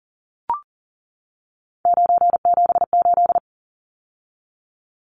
Morse code, heard as a keyed tone, about 40 wpm, 700 Hz. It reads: 978
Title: Numbers - 3-Digit 40wpm